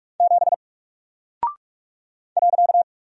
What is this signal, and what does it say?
Morse code, heard as a keyed tone, about 44 wpm, 700 Hz. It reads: 7 .
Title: Single Character (Letter:Number:Punctuation:Prosign) 45wpm